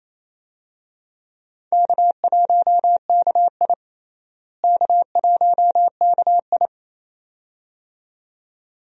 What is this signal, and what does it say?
Morse code, heard as a keyed tone, about 28 wpm, 700 Hz. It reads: K1KI K1KI
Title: USA Call Signs - 4 Characters - Repeated 2x - 28wpm